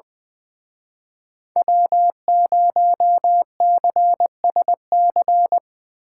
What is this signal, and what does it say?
Morse code, heard as a keyed tone, about 24 wpm, 700 Hz. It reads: W0CSC
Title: Speed Racing - USA Call Signs 20wpm